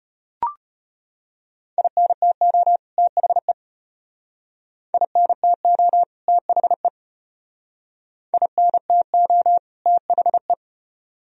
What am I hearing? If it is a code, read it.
Morse code, heard as a keyed tone, about 31 wpm, 700 Hz. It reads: INTO THE INTO THE INTO THE
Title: Speed-Racing - Sets of 2 Words 25wpm